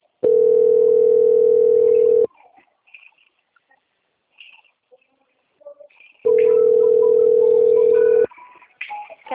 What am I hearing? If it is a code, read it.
Ringback tone (North America), 440 + 480 Hz.
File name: out- - - - - .13919